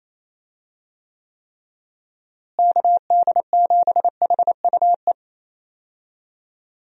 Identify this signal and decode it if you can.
Morse code, heard as a keyed tone, about 28 wpm, 700 Hz. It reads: KD7HUE
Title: USA Call Signs - 6 Characters - Repeated 3x - 28wpm